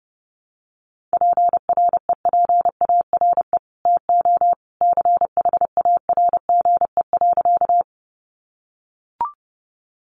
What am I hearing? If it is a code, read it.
Morse code, heard as a keyed tone, about 30 wpm, 700 Hz. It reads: PREPARE TO CHARGE.
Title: Sentences from Top 900 Words 30wpm